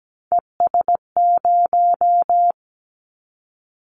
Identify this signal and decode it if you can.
Morse code, heard as a keyed tone, about 17 wpm, 700 Hz. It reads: ES0
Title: Callsign Prefixes for Top 50 DXCC - 17wpm